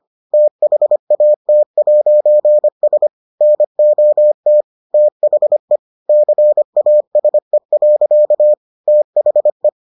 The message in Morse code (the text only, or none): THAT'S NOT THE CASE. THE